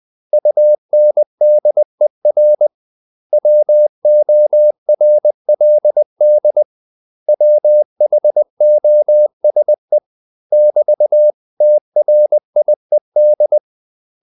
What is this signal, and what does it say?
Morse code, heard as a keyed tone, about 20 wpm, 600 Hz. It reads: UNDER WORLD WHOSE <BT>TRIED
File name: CW / Practice / 500MostCommonEnglishWords20